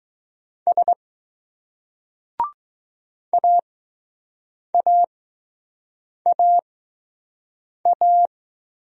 Morse code, heard as a keyed tone, about 20 wpm, spaced as for 6 wpm, 700 Hz. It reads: S AAAA